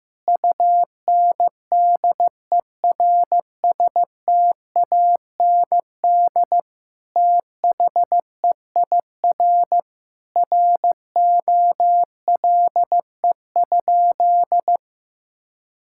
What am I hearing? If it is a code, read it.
Morse code, heard as a keyed tone, about 16 wpm, 700 Hz. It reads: UNDERSTAND THEIR ROLE?
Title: Sentences from Top 1000 Words - Review Part 6 - 15wpm